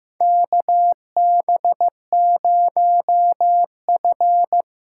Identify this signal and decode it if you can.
Morse code, heard as a keyed tone, about 15 wpm, 700 Hz. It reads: KB0F
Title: USA Call Signs - 6 Characters - 8x Character Spacing - 15wpm